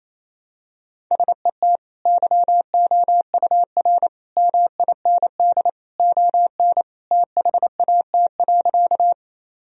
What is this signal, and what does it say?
Morse code, heard as a keyed tone, about 28 wpm, 700 Hz. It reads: SET YOUR MIND ON THAT.
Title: Sentences from Top 1000 Words - Review Part 1 - 28wpm